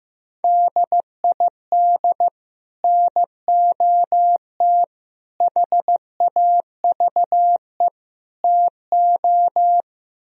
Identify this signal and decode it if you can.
Morse code, heard as a keyed tone, about 15 wpm, 700 Hz. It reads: DID NOT HAVE TO
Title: Speed-Racing - Sets of 4 Words 15wpm